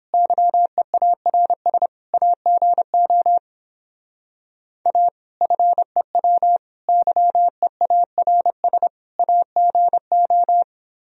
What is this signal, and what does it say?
Morse code, heard as a keyed tone, about 31 wpm, 700 Hz. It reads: YEARS AGO A FEW YEARS AGO